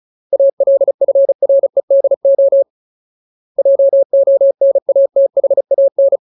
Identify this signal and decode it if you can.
Morse code, heard as a keyed tone, about 34 wpm, 550 Hz. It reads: ALFREDO JONATHAN
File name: Ultimate Word List / 35WPM Ultimate